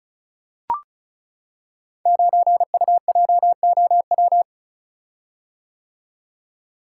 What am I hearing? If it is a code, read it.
Morse code, heard as a keyed tone, about 36 wpm, 700 Hz. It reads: 9UJOW
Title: Code Groups - 5 Letter:Numbers - 35wpm